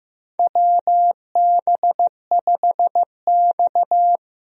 Morse code, heard as a keyed tone, about 15 wpm, 700 Hz. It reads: WB5X
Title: USA Call Signs - 4 Characters - Repeated 3x - 15wpm